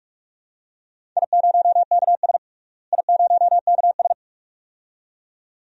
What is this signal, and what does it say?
Morse code, heard as a keyed tone, about 41 wpm, 700 Hz. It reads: I0KS I0KS
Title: DXCC-100 Call Signs - 4 Characters - Repeated 2x - 45wpm